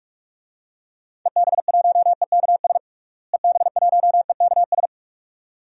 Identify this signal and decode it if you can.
Morse code, heard as a keyed tone, about 42 wpm, 700 Hz. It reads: EB1EKS EB1EKS
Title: DXCC-100 Call Signs - 6 Characters - Repeated 2x - 45wpm